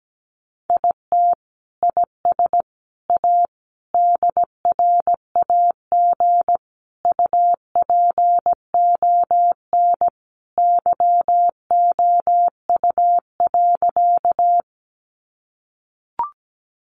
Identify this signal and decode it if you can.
Morse code, heard as a keyed tone, about 17 wpm, 700 Hz. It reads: IT IS A DRAG UPON YOU.
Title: Sentences from Top 1500 Words 17wpm